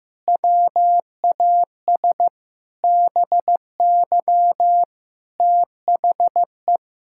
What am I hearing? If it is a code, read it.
Morse code, heard as a keyed tone, about 15 wpm, 700 Hz. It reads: WAS BY THE
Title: Sentences from Top 1600 Words 15wpm